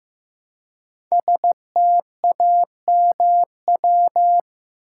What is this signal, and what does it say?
Morse code, heard as a keyed tone, about 15 wpm, 700 Hz. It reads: STAMW
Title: Code Groups - 5 Letters - 15wpm